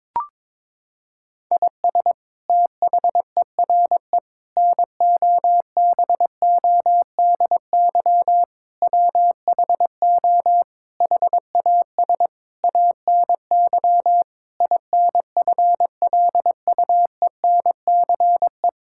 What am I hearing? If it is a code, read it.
Morse code, heard as a keyed tone, about 22 wpm, 700 Hz. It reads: IS THERE NOBODY WHO HAS ANY INFLUENCE
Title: Sentences from Top 1600 Words 22wpm